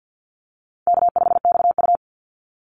Morse code, heard as a keyed tone, about 50 wpm, 700 Hz. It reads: K5XU
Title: Call Signs - 50wpm